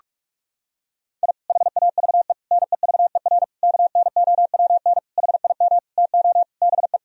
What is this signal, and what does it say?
Morse code, heard as a keyed tone, about 42 wpm, 700 Hz. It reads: I HAVE NEVER KNOWN HIM TO BE
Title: Sentences from Top 1000 Words - Review Part 3 - 45wpm